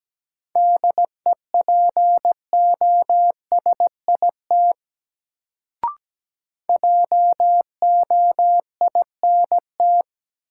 Morse code, heard as a keyed tone, about 17 wpm, 700 Hz. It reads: DEPOSIT JOINT
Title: Sentences from Top 1300 Words - New Word Drill Alt - 17wpm